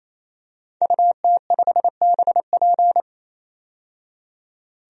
Morse code, heard as a keyed tone, about 28 wpm, 700 Hz. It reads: UT5BP
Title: DXCC-100 Call Signs - 5 Characters - 28wpm